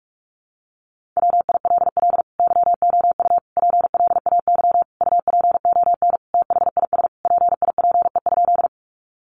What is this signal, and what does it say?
Morse code, heard as a keyed tone, about 44 wpm, 700 Hz. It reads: WILL YOU PLAY UPON THIS PIPE?
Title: Sentences from Top 1500 Words 45wpm